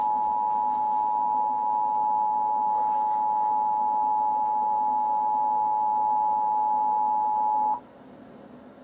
Emergency Alert System (SAME) tones (the 853 and 960 Hz attention tone). It is heard from 0.8 to 7.7 s.
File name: ORIGINAL-Emergency-Alert-System-Tornado-Warning-for-Knoxville-TN-March-2-2012